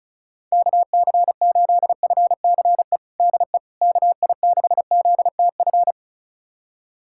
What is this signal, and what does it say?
Morse code, heard as a keyed tone, about 34 wpm, 700 Hz. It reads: KC8FCE DE KI6ZTF
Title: USA Callsign DE Callsign - 6 Characters - 35wpm